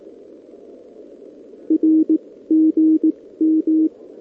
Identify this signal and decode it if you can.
Morse code, heard as a keyed tone, about 18 wpm, spaced as for 14 wpm, 320 Hz. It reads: RGM
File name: OK QRP 2026 / OKQRP2026